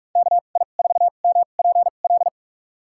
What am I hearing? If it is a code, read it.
Morse code, heard as a keyed tone, about 46 wpm, spaced as for 33 wpm, 700 Hz. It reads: KI4MPL
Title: USA Call Signs - 6 Characters - 2x Character Spacing - 45wpm